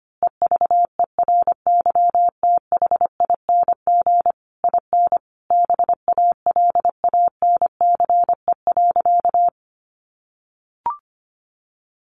Morse code, heard as a keyed tone, about 25 wpm, 700 Hz. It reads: EVERYTHING IN BALANCE.
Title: A Sound Of Thunder 25wpm